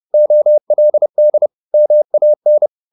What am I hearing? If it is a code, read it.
Morse code, heard as a keyed tone, about 30 wpm, 600 Hz. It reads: OLD MAN
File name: the old man and the sea / the old man and the sea 1 30wpm